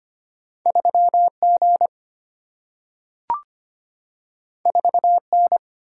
Morse code, heard as a keyed tone, about 25 wpm, 700 Hz. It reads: 3G 4N
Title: Single-Letter Single-Number 25wpm